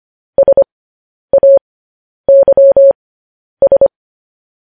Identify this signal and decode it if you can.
Morse code, heard as a keyed tone, about 25 wpm, spaced as for 10 wpm, 550 Hz. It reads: SAYS
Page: news-headlines-in-morse-code-at-10-wpm